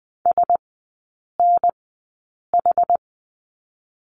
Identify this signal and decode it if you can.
Morse code, heard as a keyed tone, about 20 wpm, spaced as for 8 wpm, 700 Hz. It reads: SNH
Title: 121 Lesson 20e - Number 3 - Three Characters 20wpm